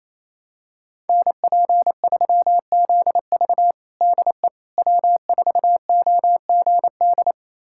Morse code, heard as a keyed tone, about 27 wpm, 700 Hz. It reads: NP3ZV DE W4OGD
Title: USA Callsign DE Callsign - 5 Characters - 28wpm